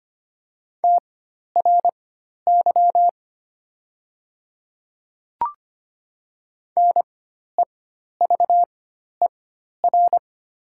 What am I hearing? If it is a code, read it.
Morse code, heard as a keyed tone, about 25 wpm, spaced as for 12 wpm, 700 Hz. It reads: TRY NEVER
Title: 4x Character Spacing - Top 500 Words - 25wpm